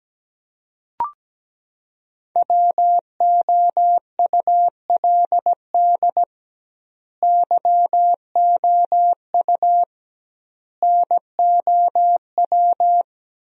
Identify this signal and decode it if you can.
Morse code, heard as a keyed tone, about 17 wpm, 700 Hz. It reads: WOULD YOU NOW
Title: 2x Word-Spacing - Sentences from Top 200 Words - 17wpm